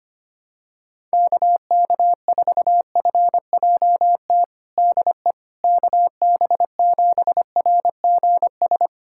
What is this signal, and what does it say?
Morse code, heard as a keyed tone, about 25 wpm, 700 Hz. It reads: KK4FJT DE KB7RGS
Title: USA Callsign DE Callsign - 6 Characters - 25wpm